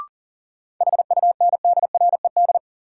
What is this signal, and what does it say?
Morse code, heard as a keyed tone, about 40 wpm, 700 Hz. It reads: HUNDRED
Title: Sentences from Top 400 Words - New Word Drill - 40wpm